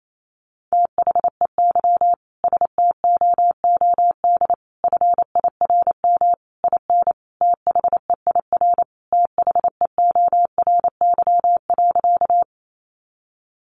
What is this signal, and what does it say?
Morse code, heard as a keyed tone, about 28 wpm, 700 Hz. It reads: THEY STOOD FIRM IN THEIR THEORY.
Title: Sentences from Top 1400 Words 28wpm